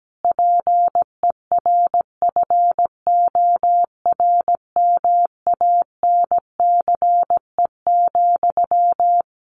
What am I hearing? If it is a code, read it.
Morse code, heard as a keyed tone, about 17 wpm, 700 Hz. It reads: PERFORMANCE,
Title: Sentences from Top 1500 Words 17wpm